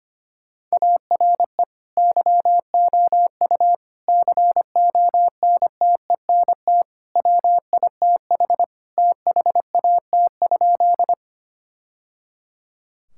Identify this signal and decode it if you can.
Morse code, heard as a keyed tone, about 25 wpm, 700 Hz. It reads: ARE YOU CONTENT WITH THAT?
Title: Sentences from Top 1000 Words - Review Part 6 - 25wpm